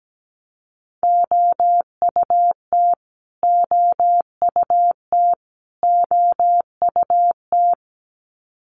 Morse code, heard as a keyed tone, about 17 wpm, 700 Hz. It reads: OUT OUT OUT
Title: Top 500 Words Repeated 17wpm